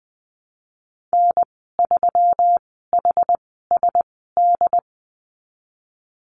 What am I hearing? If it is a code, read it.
Morse code, heard as a keyed tone, about 19 wpm, spaced as for 14 wpm, 700 Hz. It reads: N3HSD
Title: 167 Lesson 27g - Number 7 - Call Signs 20wpm